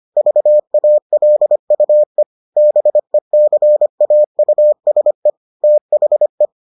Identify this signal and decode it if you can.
Morse code, heard as a keyed tone, about 25 wpm, 600 Hz. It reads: VALUE BECAUSE THE